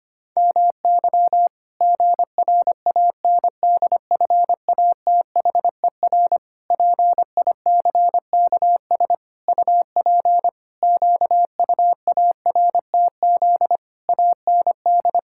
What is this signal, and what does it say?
Morse code, heard as a keyed tone, about 25 wpm, 700 Hz. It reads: MY GRANDFATHER PICKS UP QUARTZ AND
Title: Pangram Challenge v2 - 25wpm